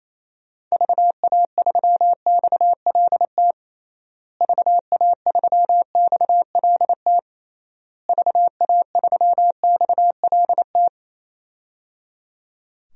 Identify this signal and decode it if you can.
Morse code, heard as a keyed tone, about 28 wpm, 700 Hz. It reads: VA3XLT VA3XLT VA3XLT
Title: DXCC-100 Call Signs - 6 Characters - Repeated 3x - 28wpm